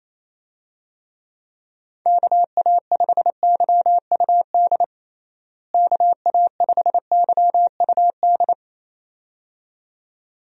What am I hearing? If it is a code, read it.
Morse code, heard as a keyed tone, about 28 wpm, 700 Hz. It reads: KA5YUD KA5YUD
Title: USA Call Signs - 6 Characters - Repeated 2x - 28wpm